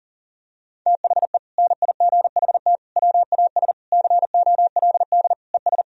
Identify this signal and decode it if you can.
Morse code, heard as a keyed tone, about 40 wpm, 700 Hz. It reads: THE NIGHT WAS COLD ES
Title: Sentences from Top 400 Words and Abbreviations 40wpm